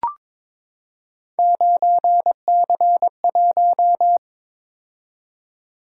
Morse code, heard as a keyed tone, about 22 wpm, 700 Hz. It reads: 9C1
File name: practice / 22wpm / Three Letter-Number Combinations - 22wpm